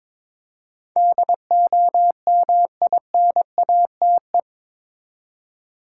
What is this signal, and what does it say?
Morse code, heard as a keyed tone, about 22 wpm, 700 Hz. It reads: DOMINATE
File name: practice / 22wpm / Eight Letter Words - 22wpm